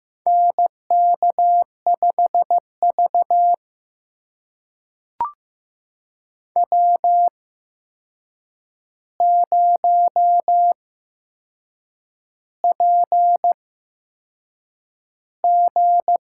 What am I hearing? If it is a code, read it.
Morse code, heard as a keyed tone, about 15 wpm, spaced as for 6 wpm, 700 Hz. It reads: NK5V W0PG
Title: USA Call Signs - 4 Characters - 8x Character Spacing - 15wpm